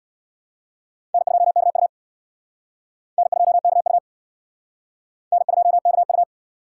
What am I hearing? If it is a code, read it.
Morse code, heard as a keyed tone, about 61 wpm, 700 Hz. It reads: N2KU N2KU N2KU
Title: Speed Racing - USA 1x2 Call Signs 50wpm